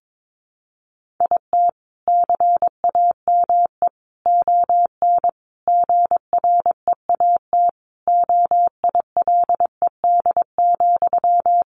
Morse code, heard as a keyed tone, about 22 wpm, 700 Hz. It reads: IT CAME ON GREAT OILED,
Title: A Sound Of Thunder 22wpm